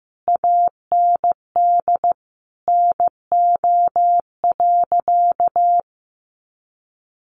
Morse code, heard as a keyed tone, about 15 wpm, 700 Hz. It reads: AND NO.
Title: Sentences from Top 500 Words 15wpm